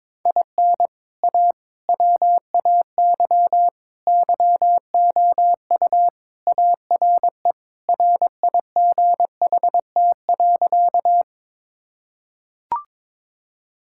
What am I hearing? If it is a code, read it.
Morse code, heard as a keyed tone, about 22 wpm, 700 Hz. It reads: IN A WAY YOU ARE RIGHT.
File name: practice / 22wpm / Sentences with 3-Word Phrases - 22wpm